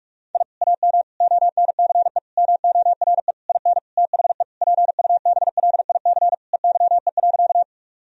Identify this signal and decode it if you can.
Morse code, heard as a keyed tone, about 46 wpm, 700 Hz. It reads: I AM ONCE MORE IN THE PUBLIC EYE.
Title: Sentences from Top 1000 Words - Review Part 3 - 45wpm